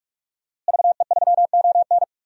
Morse code, heard as a keyed tone, about 46 wpm, 700 Hz. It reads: VE3ON
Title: DXCC-100 Call Signs - 5 Characters - Repeated 2x - 45wpm